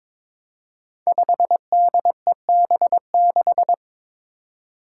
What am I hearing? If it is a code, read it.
Morse code, heard as a keyed tone, about 22 wpm, 700 Hz. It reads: 5DEB6
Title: Five Challenging Letter-Number Combinations 22wpm